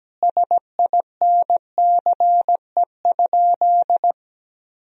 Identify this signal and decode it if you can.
Morse code, heard as a keyed tone, about 17 wpm, 700 Hz. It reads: SINCE?
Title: Sentences from Top 1000 Words - Review Part 1 - 17wpm